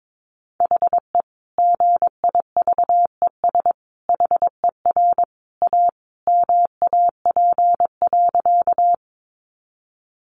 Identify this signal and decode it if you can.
Morse code, heard as a keyed tone, about 22 wpm, 700 Hz. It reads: HE GIVES HER A MAP.
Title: Additional Easy Phrases 22wpm